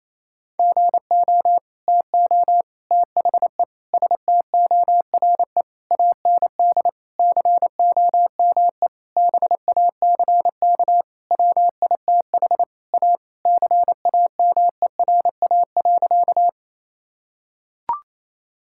Morse code, heard as a keyed tone, about 28 wpm, 700 Hz. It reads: GO TO THE STORE AND COME BACK WITH A CAMERA.
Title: Sentences from Top 1000 Words - Review Part 5 - 28wpm